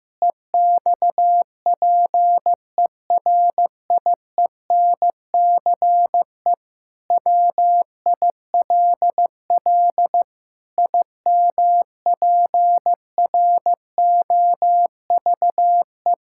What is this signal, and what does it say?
Morse code, heard as a keyed tone, about 15 wpm, 700 Hz. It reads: EXPERIENCE WILL IMPROVE
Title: Sentences from Top 1000 Words - Review Part 5 - 15wpm